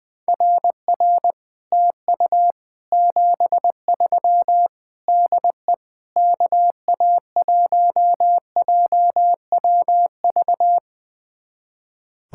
Morse code, heard as a keyed tone, about 21 wpm, 700 Hz. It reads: RR TU 73 DE KA1JWV
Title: Speed-Racing Common QSO Phrases Encore - 20wpm